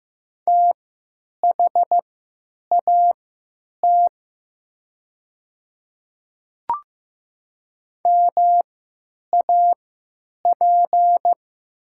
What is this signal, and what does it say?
Morse code, heard as a keyed tone, about 15 wpm, spaced as for 9 wpm, 700 Hz. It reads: THAT MAP
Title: 3x Character Spacing - Top 500 Words - 15wpm